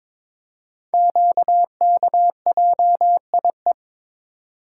Morse code, heard as a keyed tone, about 22 wpm, 700 Hz. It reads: QKJIE
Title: Code Groups - 5 Letters - 22wpm